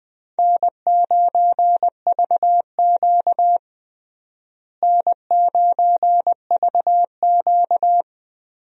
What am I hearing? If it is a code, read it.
Morse code, heard as a keyed tone, about 21 wpm, 700 Hz. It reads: N9VQ N9VQ
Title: USA Call Signs - 4 Characters - Repeated 3x - 20wpm